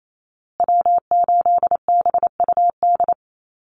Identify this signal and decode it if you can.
Morse code, heard as a keyed tone, about 28 wpm, 700 Hz. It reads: W8BUD
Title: Call Signs Repeated 2x - 28wpm